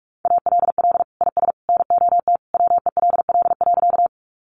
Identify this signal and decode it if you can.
Morse code, heard as a keyed tone, about 46 wpm, 700 Hz. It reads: ALL IS NOT WELL.
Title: Sentences from Top 600 Words 45wpm